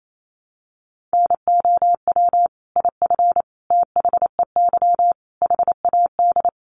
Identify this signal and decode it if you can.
Morse code, heard as a keyed tone, about 28 wpm, 700 Hz. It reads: NOW IF THEY HAD
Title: Sentences from Top 500 Words 28wpm